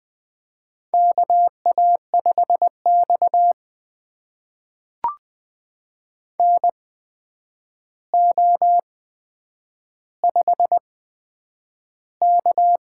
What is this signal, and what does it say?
Morse code, heard as a keyed tone, about 20 wpm, spaced as for 9 wpm, 700 Hz. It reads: KA5X NO5K
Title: USA Call Signs - 4 Characters - 8x Character Spacing - 20wpm